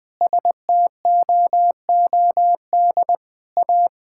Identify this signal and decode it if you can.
Morse code, heard as a keyed tone, about 20 wpm, 700 Hz. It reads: STOOD A
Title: Sentences with 4-Word Phrases - 20wpm